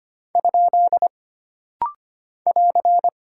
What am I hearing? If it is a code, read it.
Morse code, heard as a keyed tone, about 25 wpm, 700 Hz. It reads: ? <AR>
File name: Single Punctuation-Prosign - Rapid-Fire - 25wpm